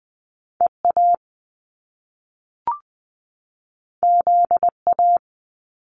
Morse code, heard as a keyed tone, about 20 wpm, 700 Hz. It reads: EA ZA
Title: Top Two Letter Combinations in Words 20wpm